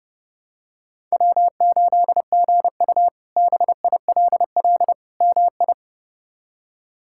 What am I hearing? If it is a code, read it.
Morse code, heard as a keyed tone, about 30 wpm, 700 Hz. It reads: W8GU BILL MI
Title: K1USN SST QSO Practice - 30wpm